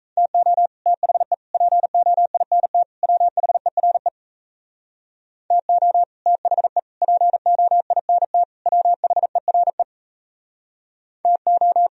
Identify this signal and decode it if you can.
Morse code, heard as a keyed tone, about 36 wpm, 700 Hz. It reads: TO THE POINT WHERE TO THE POINT WHERE TO